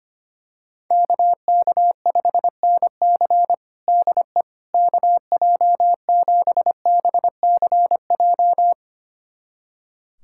Morse code, heard as a keyed tone, about 25 wpm, 700 Hz. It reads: KK5NC DE KJ7BCJ
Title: USA Callsign DE Callsign 25wpm